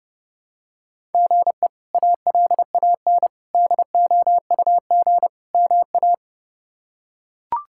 Morse code, heard as a keyed tone, about 30 wpm, 700 Hz. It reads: GE ALAN DOUG MA
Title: K1USN SST QSO Practice - 30wpm